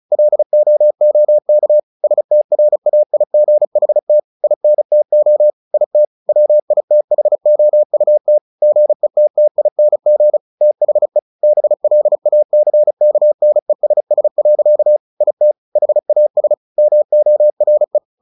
Morse code, heard as a keyed tone, about 34 wpm, 600 Hz. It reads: LOOK STRAIGHT INTO IT WITHOUT GETTING THE BLACKNESS. IT HAS MORE
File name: the old man and the sea / the old man and the sea 1 35wpm